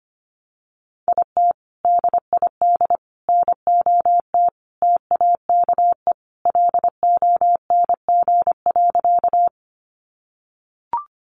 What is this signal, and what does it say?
Morse code, heard as a keyed tone, about 25 wpm, 700 Hz. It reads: IT DID NOT TAKE LONG.
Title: Sentences from Top 600 Words 25wpm